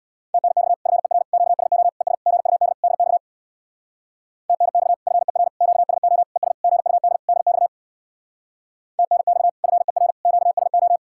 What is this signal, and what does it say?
Morse code, heard as a keyed tone, about 64 wpm, 700 Hz. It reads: TNX FER QSO ES GUD DX TNX FER QSO ES GUD DX TNX FER QSO